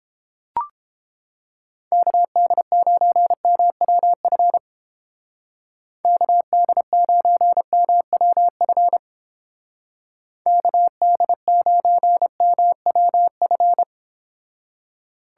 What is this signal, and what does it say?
Morse code, heard as a keyed tone, about 29 wpm, 700 Hz. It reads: KD9MWF KD9MWF KD9MWF